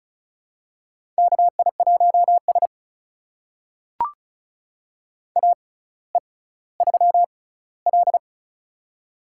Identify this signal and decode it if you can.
Morse code, heard as a keyed tone, about 36 wpm, spaced as for 18 wpm, 700 Hz. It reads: KI1S AE3L
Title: USA Call Signs - 4 Characters - 6x Character Spacing - 35wpm